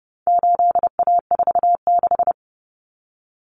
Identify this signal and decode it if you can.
Morse code, heard as a keyed tone, about 30 wpm, 700 Hz. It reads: 8A46
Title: Common Cut Numbers - 4-Digit 30wpm